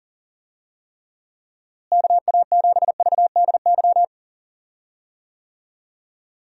Morse code, heard as a keyed tone, about 40 wpm, 700 Hz. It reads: KA7VDY
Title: USA Call Signs - 6 Characters - Repeated 2x - 40wpm